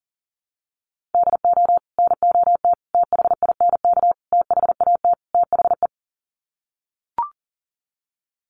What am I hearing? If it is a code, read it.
Morse code, heard as a keyed tone, about 40 wpm, 700 Hz. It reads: DO NOT THINK THAT THE
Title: Sets of 5 Words Encore2 40wpm